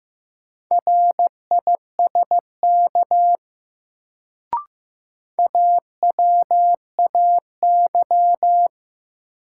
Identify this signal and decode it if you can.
Morse code, heard as a keyed tone, about 15 wpm, 700 Hz. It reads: RISK AWAY
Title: Sentences from Top 600 Words - New Word Drill - 15wpm